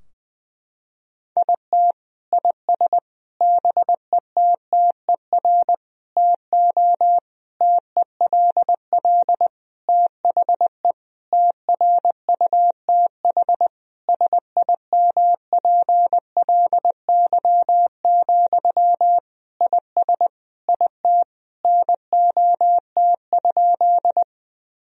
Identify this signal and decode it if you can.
Morse code, heard as a keyed tone, about 20 wpm, 700 Hz. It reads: IT IS BETTER TO TELL THE TRUTH SIMPLY, IS IT NOT?
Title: Sentences from Top 1000 Words - Review Part 4 - 20wpm